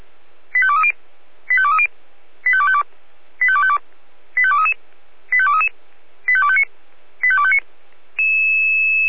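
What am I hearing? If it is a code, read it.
Five-tone selective call (ZVEI), heard as quick runs of five tones, tones of about 70 ms. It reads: 85230 85230 85252 85252 85233 85233 85259 85259
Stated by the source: Five-tone ZVEI selective call 85232 received on an SDR.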